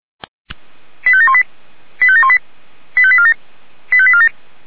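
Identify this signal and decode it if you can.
Five-tone selective call (ZVEI), heard as quick runs of five tones, tones of about 70 ms. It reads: 85718 85718 85847 85847
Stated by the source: Received on an SDR; ZVEI five-tone selective call.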